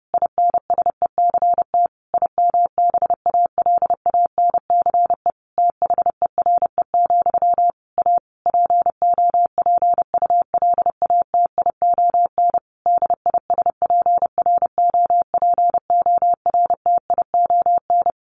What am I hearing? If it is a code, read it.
Morse code, heard as a keyed tone, about 30 wpm, 700 Hz. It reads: INSECT IMBALANCE THERE, A POPULATION DISPROPORTION
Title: A Sound Of Thunder 30wpm